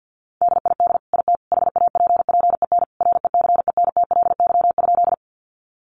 Morse code, heard as a keyed tone, about 50 wpm, 700 Hz. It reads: DID IT HAPPEN RECENTLY?
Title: Sentences from Top 1200 Words 50wpm